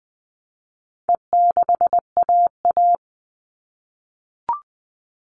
Morse code, heard as a keyed tone, about 20 wpm, 700 Hz. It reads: E6AA
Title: All Cut Numbers - 4-Digit 20wpm